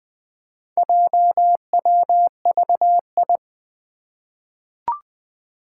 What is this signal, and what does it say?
Morse code, heard as a keyed tone, about 21 wpm, 700 Hz. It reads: JWVI
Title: Four Challenging Letter-Number Combinations 20wpm